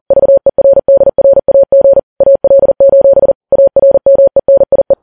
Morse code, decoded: VERDRAG AL: ARMENIE